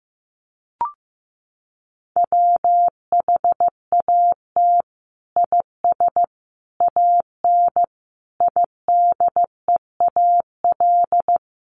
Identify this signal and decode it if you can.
Morse code, heard as a keyed tone, about 15 wpm, 700 Hz. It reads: WHAT IS AN IDEAL
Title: Sentences from Top 1400 Words 15wpm